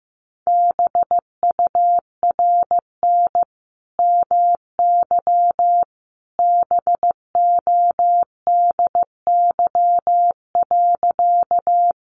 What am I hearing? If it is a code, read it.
Morse code, heard as a keyed tone, about 15 wpm, 700 Hz. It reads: BURN MY BODY.
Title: Sentences from Top 1300 Words 15wpm